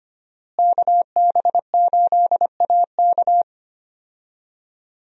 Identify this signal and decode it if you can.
Morse code, heard as a keyed tone, about 25 wpm, 700 Hz. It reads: KB8AK
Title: USA Call Signs - 5 Characters - Repeated 3x - 25wpm